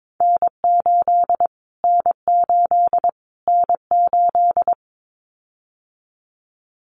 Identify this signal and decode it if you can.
Morse code, heard as a keyed tone, about 22 wpm, 700 Hz. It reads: N8 N8 N8
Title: Callsign Prefixes for Top 25 DXCC - Repeated 3x - 22wpm